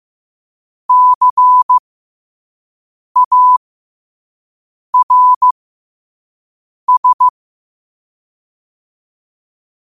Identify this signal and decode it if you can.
Morse code, heard as a keyed tone, about 15 wpm, spaced as for 5 wpm, 1000 Hz. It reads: CARS
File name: qotdfiles / 5wpm